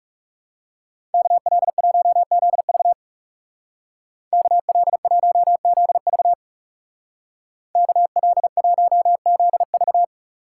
Morse code, heard as a keyed tone, about 37 wpm, 700 Hz. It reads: KL1ZV KL1ZV KL1ZV
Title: Speed Racing - USA 2x2 Call Signs 30wpm